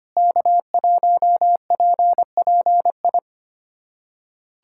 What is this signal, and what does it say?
Morse code, heard as a keyed tone, about 25 wpm, 700 Hz. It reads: K1PPI
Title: USA Callsign DE Callsign - 5 Characters - 25wpm